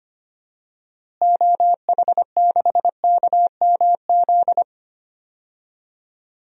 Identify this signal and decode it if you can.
Morse code, heard as a keyed tone, about 25 wpm, 700 Hz. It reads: OH6KMZ
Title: DXCC-100 Call Signs - 6 Characters - Repeated 3x - 25wpm